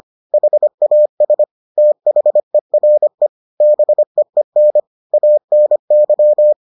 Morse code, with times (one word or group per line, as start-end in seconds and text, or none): 0.3-1.4 HAS
1.8-3.3 THERE
3.6-4.8 BEEN
5.1-6.5 ANY